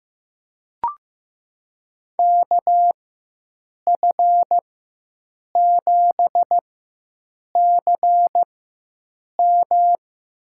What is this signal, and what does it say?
Morse code, heard as a keyed tone, about 15 wpm, spaced as for 7 wpm, 700 Hz. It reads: KF7CM
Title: USA Call Signs - 5 Characters - 4x Character Spacing - 15wpm